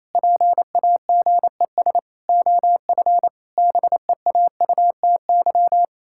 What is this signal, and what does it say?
Morse code, heard as a keyed tone, about 27 wpm, 700 Hz. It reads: PAGES OF BEAUTY
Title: Pangram Challenge v2 - 28wpm